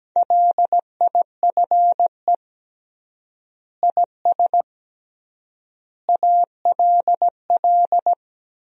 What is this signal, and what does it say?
Morse code, heard as a keyed tone, about 17 wpm, 700 Hz. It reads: LIFE IS ALL
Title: 3x Word-Spacing - Sentences from Top 300 Words - 17wpm